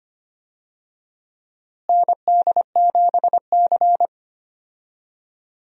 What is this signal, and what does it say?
Morse code, heard as a keyed tone, about 25 wpm, 700 Hz. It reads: ND7C
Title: USA Call Signs - 4 Characters - Repeated 3x - 25wpm